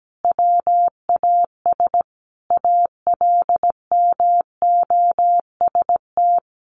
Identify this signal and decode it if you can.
Morse code, heard as a keyed tone, about 17 wpm, 700 Hz. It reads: WAS ALMOST
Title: Sentences from Top 1100 Words 17wpm